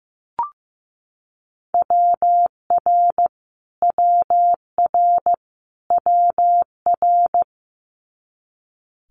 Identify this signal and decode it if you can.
Morse code, heard as a keyed tone, about 15 wpm, 700 Hz. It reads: WR WR WR
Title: Multi-letter Phonemes Repeated 3x 15wpm